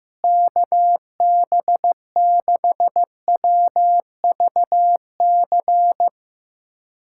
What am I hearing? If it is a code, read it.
Morse code, heard as a keyed tone, about 15 wpm, 700 Hz. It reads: KB6WVC
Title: USA Callsign DE Callsign - 6 Characters - 15wpm